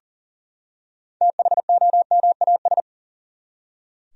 Morse code, heard as a keyed tone, about 40 wpm, 700 Hz. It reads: THOMAS
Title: USA Common Names - 40wpm